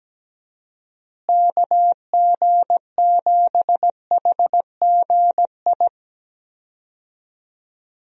Morse code, heard as a keyed tone, about 17 wpm, 700 Hz. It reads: KG7HGI
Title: USA Call Signs - 6 Characters - Repeated 2x - 17wpm